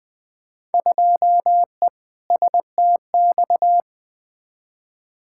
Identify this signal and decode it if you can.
Morse code, heard as a keyed tone, about 20 wpm, 700 Hz. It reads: 2E STX
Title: Field Day Exchanges Encore - 20wpm